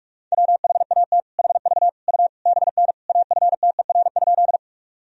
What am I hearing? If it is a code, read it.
Morse code, heard as a keyed tone, about 46 wpm, 700 Hz. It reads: WHAT HV U BN AFTER?
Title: Sentences from Top 200 Words and Abbreviations 45wpm